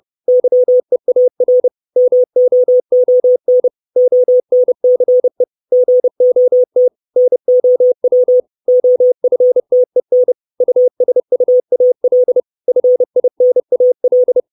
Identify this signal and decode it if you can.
Morse code, heard as a keyed tone, about 30 wpm, 500 Hz. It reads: YEAR MOON ONCE GOT NOW OFTEN USUAL FINAL
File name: 632 Words 30WPM